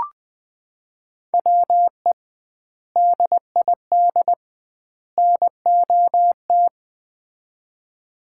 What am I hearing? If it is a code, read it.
Morse code, heard as a keyed tone, about 20 wpm, 700 Hz. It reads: WE DID NOT
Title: 2x Word-Spacing - Sets of 3 Words - 20wpm